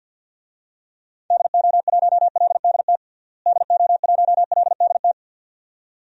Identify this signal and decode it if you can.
Morse code, heard as a keyed tone, about 52 wpm, 700 Hz. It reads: DO1LDT DO1LDT
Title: DXCC-100 Call Signs - 6 Characters - Repeated 2x - 50wpm